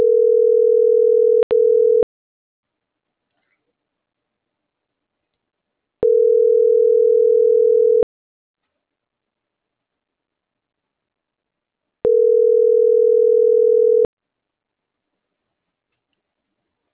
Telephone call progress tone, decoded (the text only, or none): ringback tone